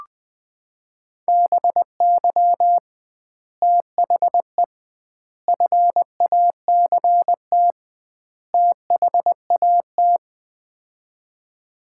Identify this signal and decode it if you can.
Morse code, heard as a keyed tone, about 20 wpm, 700 Hz. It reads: BY THE FACT THAT
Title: 2x Word-Spacing - Sets of 4 Words - 20wpm